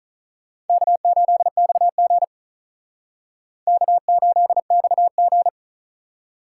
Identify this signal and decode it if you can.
Morse code, heard as a keyed tone, about 45 wpm, 700 Hz. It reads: K8XG K8XG